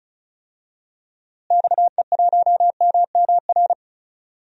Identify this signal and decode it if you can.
Morse code, heard as a keyed tone, about 36 wpm, 700 Hz. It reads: XE1MMR
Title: DXCC-100 Call Signs - 6 Characters - Repeated 3x - 35wpm